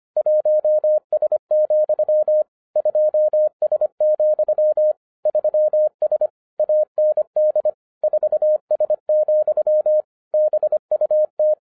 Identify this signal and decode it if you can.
Morse code, heard as a keyed tone, about 25 wpm, 610 Hz. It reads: 1S, 2S, 3S AND 4S, BUT
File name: 2012-01-mp3 / n14-FUN WITH NUMBERS-25wpm